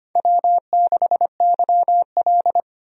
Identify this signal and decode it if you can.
Morse code, heard as a keyed tone, about 25 wpm, 700 Hz. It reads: W6YL
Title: USA Call Signs - 4 Characters - 25wpm